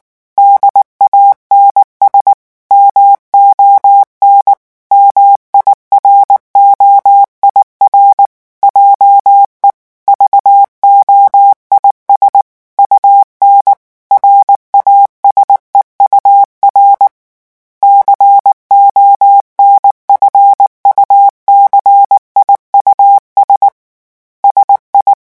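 Morse code, heard as a keyed tone, about 19 wpm, 800 Hz. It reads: DANS MON MIROIR JE VOIS UN RASEUR CONFUCIUS SI
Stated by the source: French Morse revision 8 drill of random five-letter groups at 18 words per minute.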